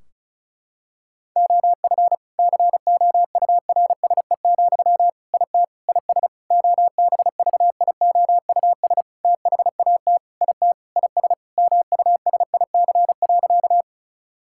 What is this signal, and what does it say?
Morse code, heard as a keyed tone, about 34 wpm, 700 Hz. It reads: OF COURSE, IT IS OBVIOUS THAT IT IS MUSIC.
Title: Sentences from Top 1600 Words 35wpm